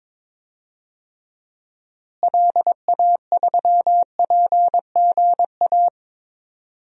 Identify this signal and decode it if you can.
Morse code, heard as a keyed tone, about 21 wpm, 700 Hz. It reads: LA3PGA
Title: DXCC-100 Call Signs - 6 Characters - Repeated 3x - 22wpm